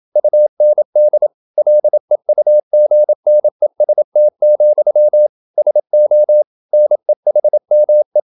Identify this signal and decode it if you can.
Morse code, heard as a keyed tone, about 27 wpm, 600 Hz. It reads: UND LEUGNEST, SO NEHME